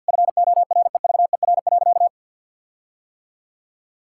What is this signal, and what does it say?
Morse code, heard as a keyed tone, about 50 wpm, 700 Hz. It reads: FOREVER.
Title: Sentences from Top 1600 Words 50wpm